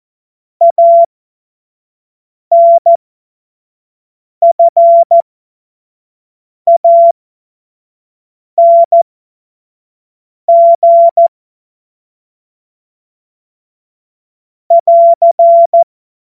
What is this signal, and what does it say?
Morse code, heard as a keyed tone, about 14 wpm, spaced as for 5 wpm, 680 Hz. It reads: ANFANG <AR>
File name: Morsekurs / Witze / Witz 24 Juni 2025 25 F70